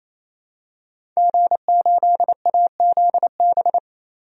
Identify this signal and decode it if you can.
Morse code, heard as a keyed tone, about 28 wpm, 700 Hz. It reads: G8AZB
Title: DXCC-100 Call Signs - 5 Characters - 28wpm